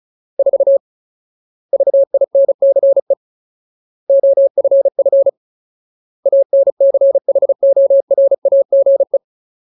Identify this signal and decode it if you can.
Morse code, heard as a keyed tone, about 35 wpm, 550 Hz. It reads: 4 VINCE OFF ANCHORAGE
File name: Ultimate Word List / 35WPM Ultimate